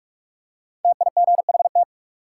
Morse code, heard as a keyed tone, about 46 wpm, 700 Hz. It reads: TIGHT
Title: Five Letter Words - 45wpm